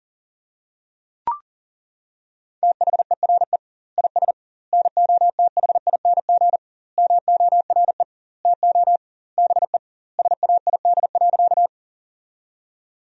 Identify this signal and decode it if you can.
Morse code, heard as a keyed tone, about 40 wpm, 700 Hz. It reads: THERE IS NOTHING MORE TO BE SAID.
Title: 2x Word-Spacing - Sentences from Top 500 Words - 40wpm